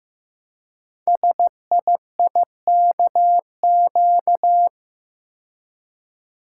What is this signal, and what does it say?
Morse code, heard as a keyed tone, about 15 wpm, 700 Hz. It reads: SIIKQ
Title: Code Groups - 5 Letters - 15wpm